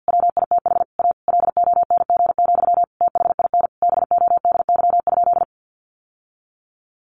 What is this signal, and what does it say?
Morse code, heard as a keyed tone, about 50 wpm, 700 Hz. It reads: WITH A LONG, THIN BODY?
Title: Sentences from Top 1200 Words 50wpm